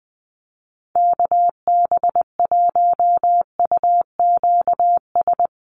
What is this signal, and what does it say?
Morse code, heard as a keyed tone, about 20 wpm, 700 Hz. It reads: KB1UQS
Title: USA Call Signs Repeated 3x 20wpm